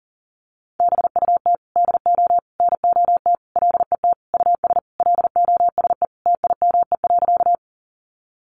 Morse code, heard as a keyed tone, about 40 wpm, 700 Hz. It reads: BUT DO NOT LET US LOSE TIME.
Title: Sentences from Top 1100 Words 40wpm